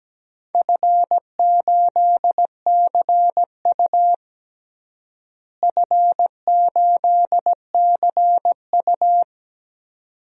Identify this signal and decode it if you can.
Morse code, heard as a keyed tone, about 17 wpm, 700 Hz. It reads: F8CU F8CU
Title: DXCC-100 Call Signs - 4 Characters - Repeated 2x - 17wpm